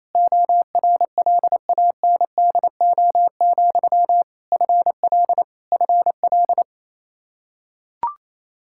Morse code, heard as a keyed tone, about 28 wpm, 700 Hz. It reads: ORLANDO, FL FL
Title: Practice QSO - Exchange 1 - Beginning - 28wpm